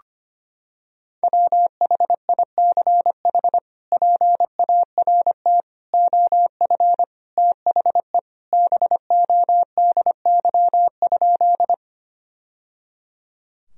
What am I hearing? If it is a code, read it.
Morse code, heard as a keyed tone, about 25 wpm, 700 Hz. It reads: WHICH PART OF THE BODY?
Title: Sentences from Top 1000 Words - Review Part 2 - 25wpm